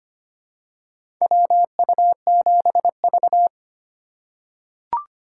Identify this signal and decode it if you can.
Morse code, heard as a keyed tone, about 25 wpm, 700 Hz. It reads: WU7V
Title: USA Call Signs - 4 Characters - 2x Character Spacing - 25wpm